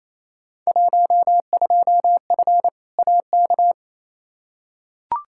Morse code, heard as a keyed tone, about 28 wpm, 700 Hz. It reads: 12F AK
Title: Field Day Exchanges Encore - 28wpm